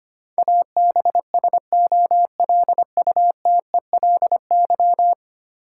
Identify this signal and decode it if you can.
Morse code, heard as a keyed tone, about 25 wpm, 700 Hz. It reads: ABSOLUTELY
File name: Ten Letter Words - 25wpm